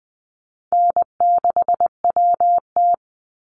Morse code, heard as a keyed tone, about 20 wpm, 700 Hz. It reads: N6WT
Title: NA SOTA Chaser Callsigns 20wpm